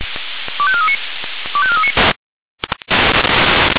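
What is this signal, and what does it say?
Five-tone selective call (ZVEI), heard as quick runs of five tones, tones of about 70 ms. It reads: 25439 25439
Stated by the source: ZVEI five-tone selective call 25419 received by SDR.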